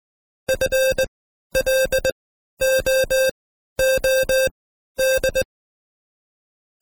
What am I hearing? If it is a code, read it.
Morse code, heard as a keyed tone, about 19 wpm, spaced as for 12 wpm, 530 Hz. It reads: FLOOD
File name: Code Quick / AudioDownld2 / Seg132-Stepper5 11WPM